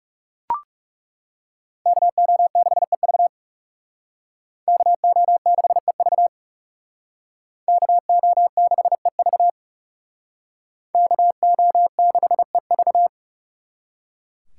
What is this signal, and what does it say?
Morse code, heard as a keyed tone, about 38 wpm, 700 Hz. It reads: KO6EV KO6EV KO6EV KO6EV